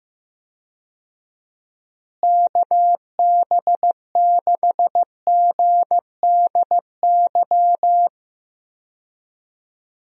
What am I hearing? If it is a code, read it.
Morse code, heard as a keyed tone, about 16 wpm, 700 Hz. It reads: KB6GDY
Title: USA Call Signs - 6 Characters - Repeated 3x - 15wpm